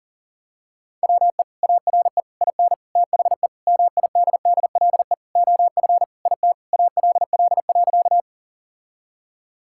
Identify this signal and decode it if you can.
Morse code, heard as a keyed tone, about 37 wpm, 700 Hz. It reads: WE ARE IN THE MIDDLE OF IT ALL.
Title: Sentences from Top 1000 Words - Review Part 6 - 40wpm